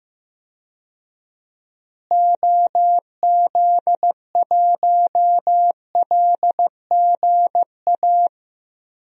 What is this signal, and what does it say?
Morse code, heard as a keyed tone, about 15 wpm, 700 Hz. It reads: OZ1LGA
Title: DXCC-100 Call Signs - 6 Characters - Repeated 2x - 15wpm